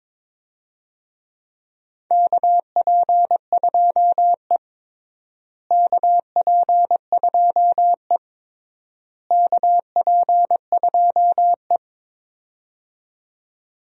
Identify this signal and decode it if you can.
Morse code, heard as a keyed tone, about 22 wpm, 700 Hz. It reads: KP2E KP2E KP2E
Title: DXCC-100 Call Signs - 4 Characters - Repeated 3x - 22wpm